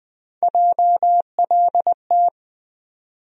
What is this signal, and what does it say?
Morse code, heard as a keyed tone, about 20 wpm, 700 Hz. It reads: JLT
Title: USA Call Sign Postfixes - 3 Letters 20wpm